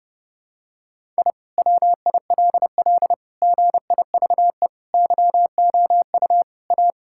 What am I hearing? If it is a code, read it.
Morse code, heard as a keyed tone, about 30 wpm, 700 Hz. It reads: I WILL GIVE YOU A
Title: Sentences from Top 1000 Words - Review Part 6 - 30wpm